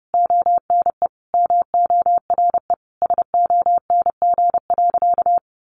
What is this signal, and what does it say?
Morse code, heard as a keyed tone, about 29 wpm, 700 Hz. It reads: ONE MORE SONG.
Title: Sentences from Top 400 Words 30wpm